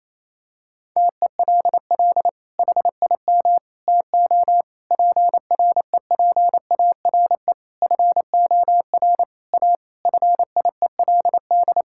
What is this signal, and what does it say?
Morse code, heard as a keyed tone, about 28 wpm, 700 Hz. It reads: TELL HIM TO PREPARE FOR A FIELD
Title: Sentences from Top 1000 Words - Review Part 6 - 28wpm